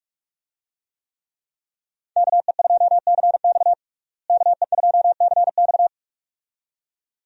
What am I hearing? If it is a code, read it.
Morse code, heard as a keyed tone, about 44 wpm, 700 Hz. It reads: KE2CX KE2CX
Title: USA Call Signs - 5 Characters - Repeated 2x - 45wpm